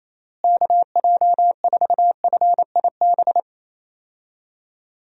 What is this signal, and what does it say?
Morse code, heard as a keyed tone, about 28 wpm, 700 Hz. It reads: KJ4FIB
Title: USA Call Signs - 6 Characters - Repeated 3x - 28wpm